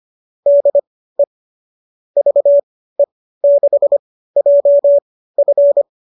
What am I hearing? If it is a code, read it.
Morse code, heard as a keyed tone, about 25 wpm, spaced as for 15 wpm, 570 Hz. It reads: DE VE6JF
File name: qso203 15